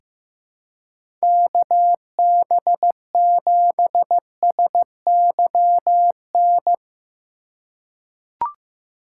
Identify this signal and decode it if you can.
Morse code, heard as a keyed tone, about 15 wpm, 700 Hz. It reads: KB7SYN
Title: USA Call Signs - 6 Characters - 3x Character Spacing - 15wpm